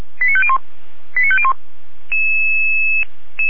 Five-tone selective call (ZVEI), heard as quick runs of five tones, tones of about 70 ms. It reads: 89501 89501
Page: audio.f85360.feuerwehr-orlamuende.org